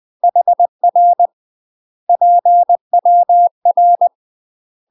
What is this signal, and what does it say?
Morse code, heard as a keyed tone, about 20 wpm, 700 Hz. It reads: HR PWR
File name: UK MASTER QSOs TRAINING / 20WPM / UK MASTER QSOs 0010